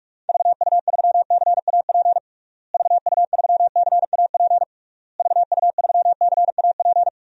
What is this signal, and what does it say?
Morse code, heard as a keyed tone, about 46 wpm, 700 Hz. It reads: VU3CAP VU3CAP VU3CAP
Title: DXCC-100 Call Signs - 6 Characters - Repeated 3x - 45wpm